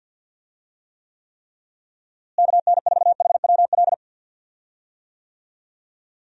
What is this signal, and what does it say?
Morse code, heard as a keyed tone, about 46 wpm, 700 Hz. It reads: KN4HWL
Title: USA Call Signs - 6 Characters - Repeated 3x - 50wpm